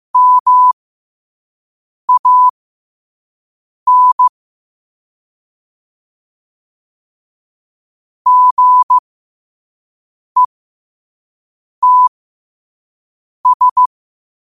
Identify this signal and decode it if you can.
Morse code, heard as a keyed tone, about 15 wpm, spaced as for 5 wpm, 1000 Hz. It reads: MAN GETS
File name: qotdfiles / 5wpm